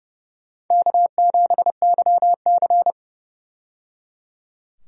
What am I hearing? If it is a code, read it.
Morse code, heard as a keyed tone, about 29 wpm, 700 Hz. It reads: K7YC
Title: USA Call Signs - 4 Characters - 30wpm